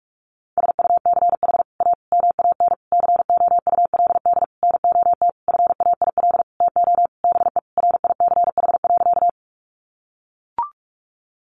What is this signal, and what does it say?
Morse code, heard as a keyed tone, about 44 wpm, 700 Hz. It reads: SUCH A MAN COULD NOT FAIL TO BE RICH.
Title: Sentences from Top 1100 Words 45wpm